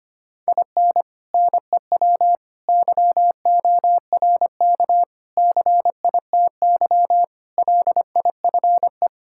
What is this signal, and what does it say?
Morse code, heard as a keyed tone, about 25 wpm, 700 Hz. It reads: IN NEW YORK CITY LIFE
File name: practice / 25wpm / Sentences with 4-Word Phrases - 25wpm